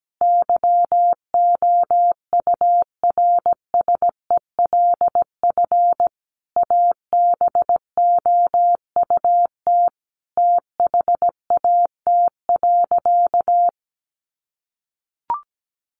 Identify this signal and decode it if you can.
Morse code, heard as a keyed tone, about 17 wpm, 700 Hz. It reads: YOURSELF ABOUT THAT.
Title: Sentences from Top 1200 Words 17wpm